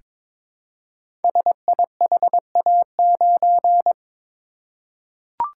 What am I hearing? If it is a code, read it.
Morse code, heard as a keyed tone, about 22 wpm, 700 Hz. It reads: SIHA9
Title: Code Groups - 5 Letter:Numbers - 22wpm